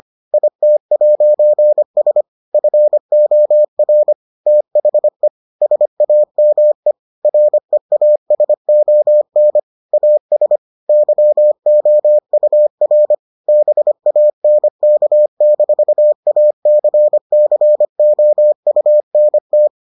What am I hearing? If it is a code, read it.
Morse code, heard as a keyed tone, about 25 wpm, 600 Hz. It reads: IT'S FOR THE SAME REASON AS YOUR BANK-ACCOUNT